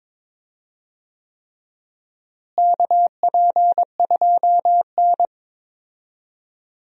Morse code, heard as a keyed tone, about 22 wpm, 700 Hz. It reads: KP2N
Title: DXCC-100 Call Signs - 4 Characters - Repeated 3x - 22wpm